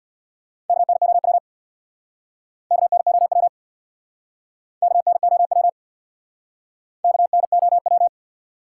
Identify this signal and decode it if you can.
Morse code, heard as a keyed tone, about 62 wpm, 700 Hz. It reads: KNOW KNOW KNOW KNOW
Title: Speed-Racing Top 180 QSO Elements 50wpm